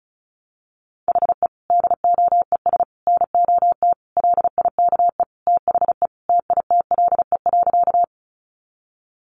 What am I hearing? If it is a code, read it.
Morse code, heard as a keyed tone, about 34 wpm, 700 Hz. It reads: HE DOES NOT LIKE THE TITLE.
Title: Sentences from Top 1200 Words 35wpm